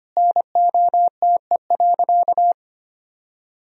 Morse code, heard as a keyed tone, about 25 wpm, 700 Hz. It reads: NOTE.
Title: Sentences from Top 1000 Words - Review Part 2 - 25wpm